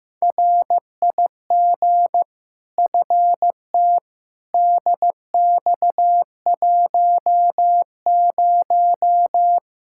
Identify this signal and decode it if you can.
Morse code, heard as a keyed tone, about 15 wpm, 700 Hz. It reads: RIG FT DX10
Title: Practice QSO - Exchange 2 - Info - 15wpm